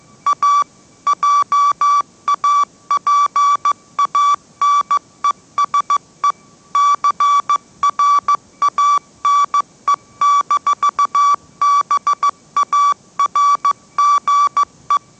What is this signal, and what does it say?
Morse code, heard as a keyed tone, about 15 wpm, 1150 Hz. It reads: A JAPANESE CRANE-BARGE
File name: mrx / mrx065 16wpm